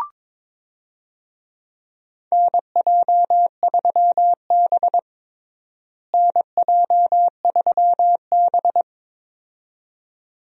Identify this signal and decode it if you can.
Morse code, heard as a keyed tone, about 21 wpm, 700 Hz. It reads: NJ3B NJ3B
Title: USA Call Signs - 4 Characters - Repeated 2x - 22wpm